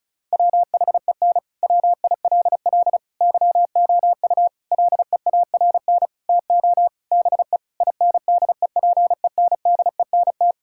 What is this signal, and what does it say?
Morse code, heard as a keyed tone, about 36 wpm, 700 Hz. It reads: WHEN WILL YOU LEARN TO BE INDEPENDENT
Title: Sentences from Top 1000 Words - Review Part 6 - 35wpm